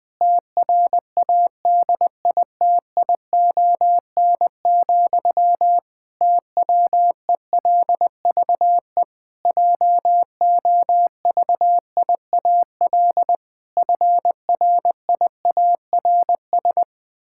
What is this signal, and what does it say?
Morse code, heard as a keyed tone, about 20 wpm, 700 Hz. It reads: TRADITION, TWELVE JOVIAL FRIARS
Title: Pangram Challenge v2 - 20wpm